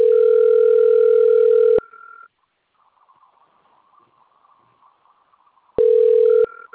Ringback tone (North America), 440 + 480 Hz.
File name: rg-1600-unknown- - - .43566